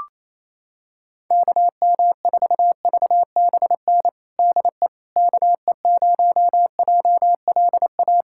Morse code, heard as a keyed tone, about 27 wpm, 700 Hz. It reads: KM4VBN DE KE0JLA
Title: USA Callsign DE Callsign - 6 Characters - 28wpm